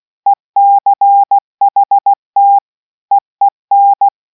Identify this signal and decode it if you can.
Morse code, heard as a keyed tone, about 16 wpm, 800 Hz. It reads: ECHT EEN